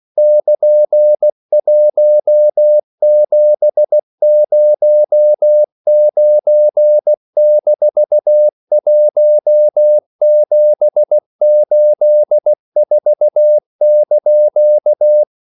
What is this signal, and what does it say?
Morse code, heard as a keyed tone, about 16 wpm, 600 Hz. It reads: <KN>1709-1784)
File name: CW / Practice / EnglishQuotations0216